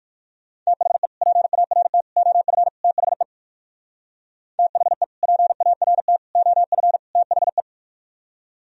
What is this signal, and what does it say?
Morse code, heard as a keyed tone, about 58 wpm, 700 Hz. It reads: THE PART OF THE THE PART OF THE